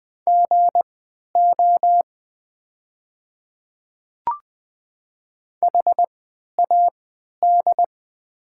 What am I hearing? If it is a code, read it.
Morse code, heard as a keyed tone, about 21 wpm, spaced as for 12 wpm, 700 Hz. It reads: GO HAD
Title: 3x Character Spacing - Top 100 Words - 20wpm